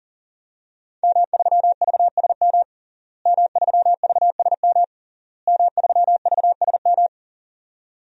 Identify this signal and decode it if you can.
Morse code, heard as a keyed tone, about 40 wpm, 700 Hz. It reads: M3VSM M3VSM M3VSM
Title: DXCC-100 Call Signs - 5 Characters - Repeated 3x - 40wpm